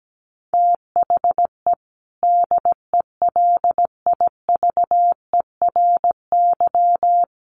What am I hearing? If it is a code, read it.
Morse code, heard as a keyed tone, about 17 wpm, 700 Hz. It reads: THE DELIVERY
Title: Sentences from Top 1500 Words 17wpm